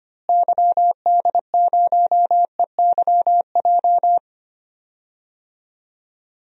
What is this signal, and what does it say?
Morse code, heard as a keyed tone, about 25 wpm, 700 Hz. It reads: YD0EYJ
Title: DXCC-100 Call Signs - 6 Characters - Repeated 2x - 25wpm